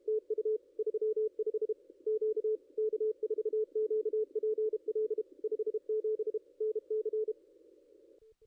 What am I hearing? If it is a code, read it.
Morse code, heard as a keyed tone, about 31 wpm, 440 Hz. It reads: TU 35 Q K4QPL 57 NC
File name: contests / ss-cw-k5tr-op-n5rz -2200